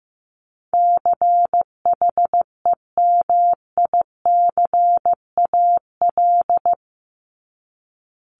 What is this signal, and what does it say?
Morse code, heard as a keyed tone, about 15 wpm, 700 Hz. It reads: CHEMICAL
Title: Top 1001-1250 Words 15wpm